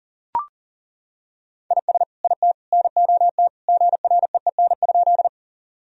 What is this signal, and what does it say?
Morse code, heard as a keyed tone, about 37 wpm, 700 Hz. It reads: IS IT NOT GREEN?
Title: Sentences from Top 1000 Words - Review Part 2 - 40wpm